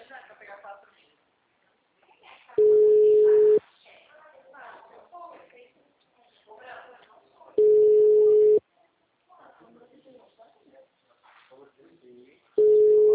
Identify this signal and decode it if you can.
Ringback tone (Europe, 425 Hz), with speech.
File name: out- -2015- - - .9990